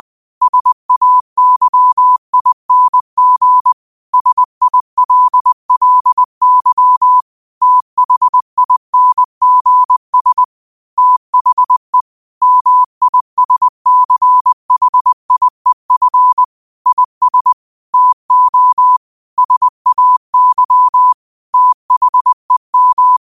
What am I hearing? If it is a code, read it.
Morse code, heard as a keyed tone, about 20 wpm, 1000 Hz. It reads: SAYING SILLY THINGS THE MISCHIEF IS TO SAY THEM
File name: qotdfiles / 20wpm